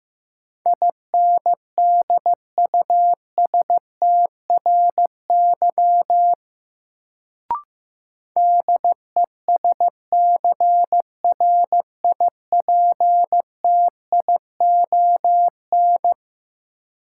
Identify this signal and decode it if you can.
Morse code, heard as a keyed tone, about 15 wpm, 700 Hz. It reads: INDUSTRY DESCRIPTION
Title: Sentences from Top 1300 Words - New Word Drill Alt - 15wpm